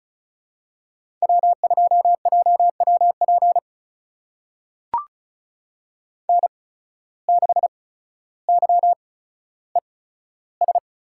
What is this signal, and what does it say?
Morse code, heard as a keyed tone, about 34 wpm, spaced as for 15 wpm, 690 Hz. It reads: W2JWP N6YES
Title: USA Call Signs - 5 Characters - 8x Character Spacing - 35wpm